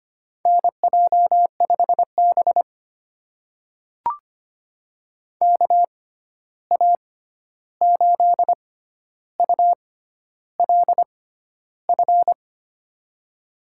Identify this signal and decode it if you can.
Morse code, heard as a keyed tone, about 25 wpm, spaced as for 11 wpm, 700 Hz. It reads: NJ5B KA8ULF
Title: USA Call Signs - 6x Character Spacing - 25wpm